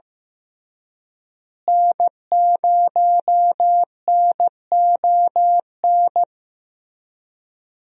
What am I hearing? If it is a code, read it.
Morse code, heard as a keyed tone, about 15 wpm, 700 Hz. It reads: N0NON
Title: USA Call Signs - 5 Characters - Repeated 3x - 15wpm